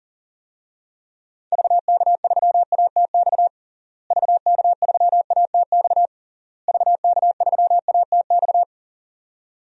Morse code, heard as a keyed tone, about 37 wpm, 700 Hz. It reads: VK3ATX VK3ATX VK3ATX
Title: DXCC-100 Call Signs - 6 Characters - Repeated 3x - 40wpm